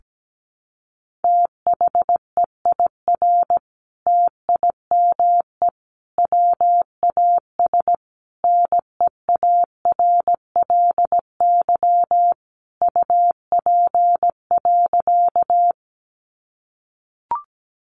Morse code, heard as a keyed tone, about 17 wpm, 700 Hz. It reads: THEIR TIME WAS NEARLY UP.
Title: Sentences from Top 1300 Words 17wpm